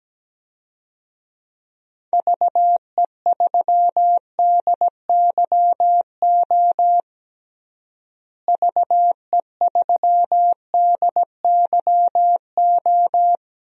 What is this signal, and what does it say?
Morse code, heard as a keyed tone, about 17 wpm, 700 Hz. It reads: VE3DYO VE3DYO
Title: DXCC-100 Call Signs - 6 Characters - Repeated 3x - 17wpm